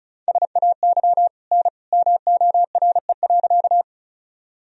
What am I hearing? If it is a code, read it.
Morse code, heard as a keyed tone, about 36 wpm, 700 Hz. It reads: SAY N MORE.
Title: Sentences from Top 200 Words and Abbreviations 35wpm